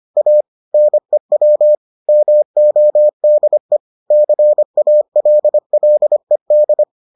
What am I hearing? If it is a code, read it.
Morse code, heard as a keyed tone, about 25 wpm, 600 Hz. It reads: A NEW MODE CALLED